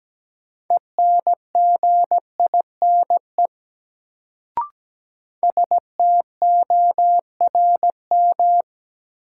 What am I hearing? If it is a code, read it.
Morse code, heard as a keyed tone, about 17 wpm, 700 Hz. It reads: ENGINE STORM
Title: Sentences from Top 1500 Words - New Word Drill Alt - 17wpm